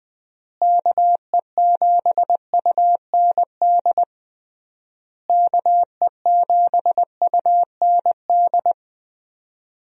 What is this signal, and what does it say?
Morse code, heard as a keyed tone, about 20 wpm, 700 Hz. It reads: KE7UND KE7UND
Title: USA Call Signs - 6 Characters - Repeated 3x - 20wpm